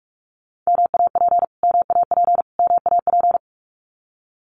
Morse code, heard as a keyed tone, about 44 wpm, 700 Hz. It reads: MAP MAP MAP
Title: Top 500 Words Repeated 45wpm